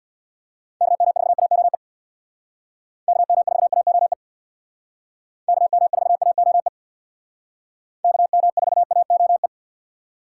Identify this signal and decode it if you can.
Morse code, heard as a keyed tone, about 64 wpm, 700 Hz. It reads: KM4AOE KM4AOE KM4AOE KM4AOE